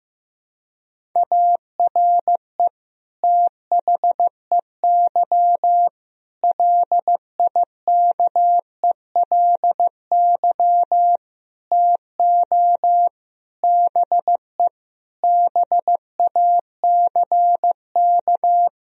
Morse code, heard as a keyed tone, about 15 wpm, 700 Hz. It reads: ARE THEY LIKELY TO BE BACK
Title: Sentences from Top 1000 Words - Review Part 4 - 15wpm